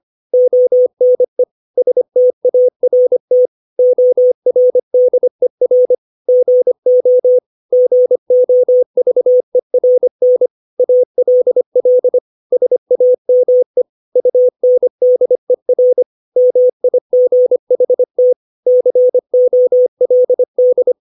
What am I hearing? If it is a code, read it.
Morse code, heard as a keyed tone, about 25 wpm, 500 Hz. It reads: ONE START ORDER GO GOVERN ALL SAME UNDER MIGHT COLD